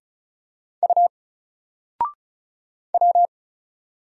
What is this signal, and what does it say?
Morse code, heard as a keyed tone, about 36 wpm, 700 Hz. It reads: U W
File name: Single-Letter-Number-35wpm